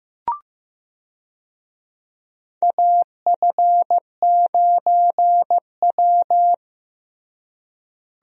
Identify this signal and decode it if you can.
Morse code, heard as a keyed tone, about 15 wpm, 700 Hz. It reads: AF9W
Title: USA Call Signs - 4 Characters - Repeated 3x - 15wpm